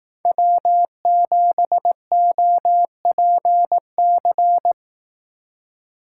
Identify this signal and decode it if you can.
Morse code, heard as a keyed tone, about 18 wpm, 700 Hz. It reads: W7OPC
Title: Speed Racing - USA 1x3 Call Signs 15wpm